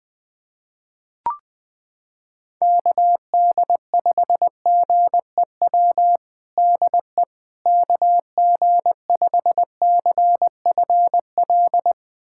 Morse code, heard as a keyed tone, about 21 wpm, 700 Hz. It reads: KD5GEW DE KG5CFL
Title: USA Callsign DE Callsign - 6 Characters - 20wpm